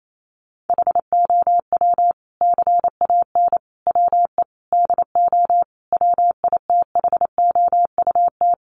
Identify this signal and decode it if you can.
Morse code, heard as a keyed tone, about 28 wpm, 700 Hz. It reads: HOW CAN WE DO WITHOUT
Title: Sentences from Top 600 Words 28wpm